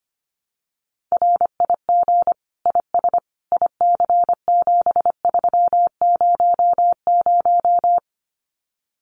Morse code, heard as a keyed tone, about 25 wpm, 700 Hz. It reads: RIG IS IC7300
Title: Common QSO Phrases Encore 25wpm